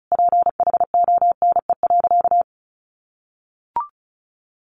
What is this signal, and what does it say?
Morse code, heard as a keyed tone, about 34 wpm, 700 Hz. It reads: PHONE.
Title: Sentences from Top 1100 Words 35wpm